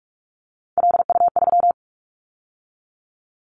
Morse code, heard as a keyed tone, about 46 wpm, 700 Hz. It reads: LU3
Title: Callsign Prefixes for Top 100 DXCC - 45wpm